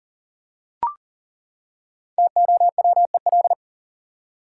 Morse code, heard as a keyed tone, about 40 wpm, 700 Hz. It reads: TOWEL
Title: Five Letter Words - 40wpm